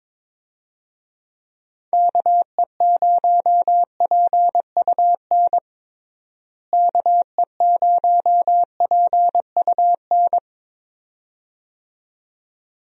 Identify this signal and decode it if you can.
Morse code, heard as a keyed tone, about 21 wpm, 700 Hz. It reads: KE0PUN KE0PUN
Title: USA Call Signs - 6 Characters - Repeated 2x - 22wpm